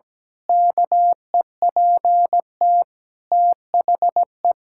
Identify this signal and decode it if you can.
Morse code, heard as a keyed tone, about 17 wpm, 700 Hz. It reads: KEPT THE
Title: Pangram Challenge v2 - 17wpm